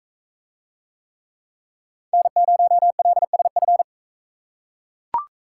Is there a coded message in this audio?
Yes: Morse code from 2.1 to 3.8 s, heard as a keyed tone, about 40 wpm, 700 Hz. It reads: N0LSF